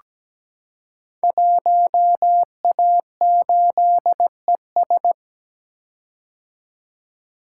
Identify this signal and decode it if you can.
Morse code, heard as a keyed tone, about 17 wpm, 700 Hz. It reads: 1A8ES
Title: Code Groups - 5 Letter:Numbers - 17wpm